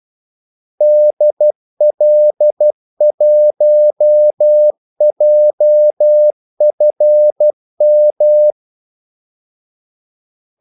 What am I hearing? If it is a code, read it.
Morse code, heard as a keyed tone, about 12 wpm, 600 Hz. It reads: DL1JFM
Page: 2 ekstra ordmellemrum mellem kaldesignalerne